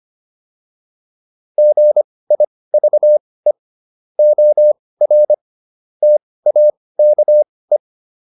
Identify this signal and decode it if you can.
Morse code, heard as a keyed tone, about 25 wpm, spaced as for 18 wpm, 600 Hz. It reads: GIVE OR TAKE